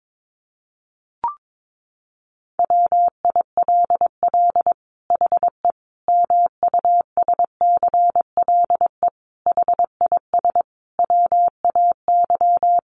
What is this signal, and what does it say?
Morse code, heard as a keyed tone, about 22 wpm, 700 Hz. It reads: WILL HE MUSCLE HIS WAY
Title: Sentences from Top 1300 Words 22wpm